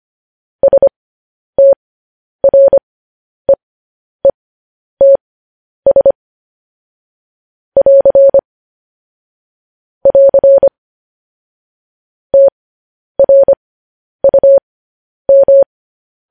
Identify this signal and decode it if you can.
Morse code, heard as a keyed tone, about 25 wpm, spaced as for 10 wpm, 550 Hz. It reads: STREETS <AR> <AR> TRUM
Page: News Headlines in Morse Code at 10 WPM